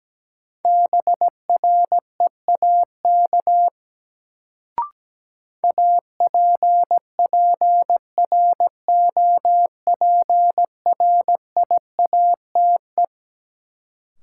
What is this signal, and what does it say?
Morse code, heard as a keyed tone, about 17 wpm, 700 Hz. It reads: BREAK APPROPRIATE
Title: Sentences from Top 900 Words - New Word Drill - 17wpm